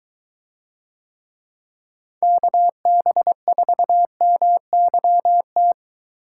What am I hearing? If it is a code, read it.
Morse code, heard as a keyed tone, about 23 wpm, 700 Hz. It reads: KB4MYT